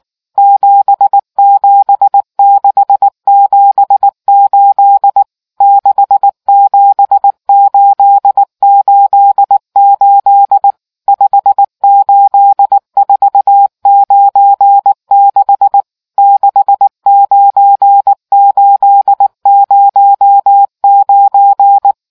Morse code, heard as a keyed tone, about 19 wpm, 800 Hz. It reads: 77678 67888 58496 69809